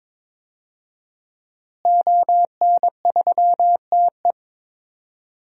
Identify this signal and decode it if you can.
Morse code, heard as a keyed tone, about 22 wpm, 700 Hz. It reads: ON3TE
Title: DXCC-100 Call Signs - 5 Characters - Repeated 2x - 22wpm